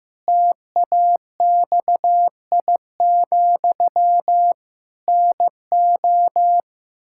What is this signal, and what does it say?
Morse code, heard as a keyed tone, about 15 wpm, 700 Hz. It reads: TAXI, NO
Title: Pangram Challenge v2 - 15wpm